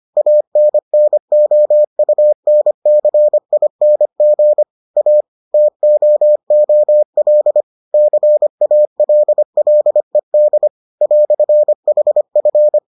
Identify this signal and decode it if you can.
Morse code, heard as a keyed tone, about 25 wpm, 600 Hz. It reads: ANNOUNCING A TOOL CALLED "HF